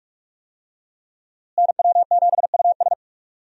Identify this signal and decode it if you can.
Morse code, heard as a keyed tone, about 43 wpm, 700 Hz. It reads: NW7US
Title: USA Call Signs - 5 Characters - Repeated 3x - 45wpm